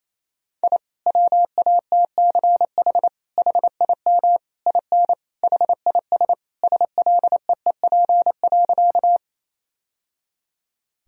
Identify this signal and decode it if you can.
Morse code, heard as a keyed tone, about 28 wpm, 700 Hz. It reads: I WATCH HIM IN HIS SLEEP.
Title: Sentences from Top 1000 Words - Review Part 1 - 28wpm